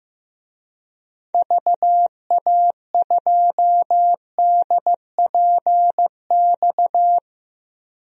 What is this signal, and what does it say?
Morse code, heard as a keyed tone, about 15 wpm, 700 Hz. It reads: VA2DPX
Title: DXCC-100 Call Signs - 6 Characters - 15wpm